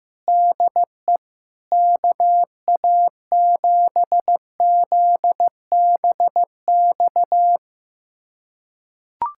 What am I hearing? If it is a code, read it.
Morse code, heard as a keyed tone, about 16 wpm, 700 Hz. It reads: DE KA7ZBX
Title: USA Callsign DE Callsign - 6 Characters - 15wpm